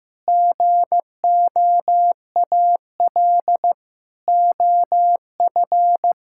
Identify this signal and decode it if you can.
Morse code, heard as a keyed tone, about 15 wpm, 700 Hz. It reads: GOAL OF
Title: Sentences from Top 1000 Words - Review Part 5 - 15wpm